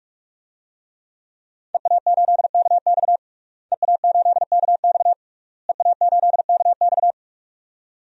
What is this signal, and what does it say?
Morse code, heard as a keyed tone, about 46 wpm, 700 Hz. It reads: EA8KX EA8KX EA8KX
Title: DXCC-100 Call Signs - 5 Characters - Repeated 3x - 45wpm